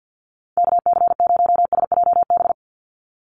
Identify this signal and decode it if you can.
Morse code, heard as a keyed tone, about 50 wpm, 700 Hz. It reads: KC0SJB
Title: USA Call Signs 50wpm